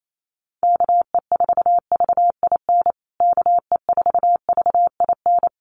Morse code, heard as a keyed tone, about 28 wpm, 700 Hz. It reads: KE4VIN KE4VIN
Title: USA Call Signs Repeated 3x 28wpm